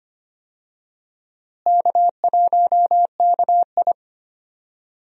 Morse code, heard as a keyed tone, about 25 wpm, 700 Hz. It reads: K1KI
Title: USA Call Signs - 4 Characters - Repeated 2x - 25wpm